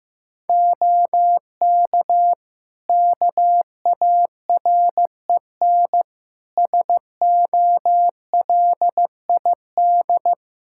Morse code, heard as a keyed tone, about 15 wpm, 700 Hz. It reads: OK KAREN SOLID
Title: Practice QSO - Exchange 3 - More Info - 15wpm